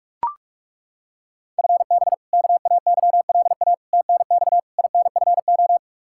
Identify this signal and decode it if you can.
Morse code, heard as a keyed tone, about 42 wpm, 700 Hz. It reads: FB KAYLA TNX INFO
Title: K1USN SST QSO Practice 45wpm